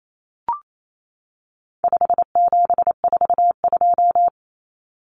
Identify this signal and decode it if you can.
Morse code, heard as a keyed tone, about 28 wpm, 700 Hz. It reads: 5742
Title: Common Cut Numbers - 4-Digit 28wpm